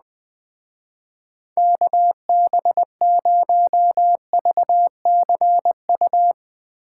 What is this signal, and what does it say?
Morse code, heard as a keyed tone, about 19 wpm, 700 Hz. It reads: KB0VCU
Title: Speed Racing - USA 2x3 Call Signs 17wpm